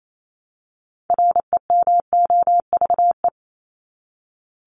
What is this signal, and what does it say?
Morse code, heard as a keyed tone, about 28 wpm, 700 Hz. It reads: REMOVE
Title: ED Suffix - Top 100 Words with Base Words 28wpm